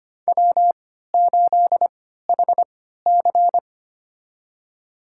Morse code, heard as a keyed tone, about 25 wpm, spaced as for 14 wpm, 700 Hz. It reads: W8HC
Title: USA Call Signs - 4 Characters - 3x Character Spacing - 25wpm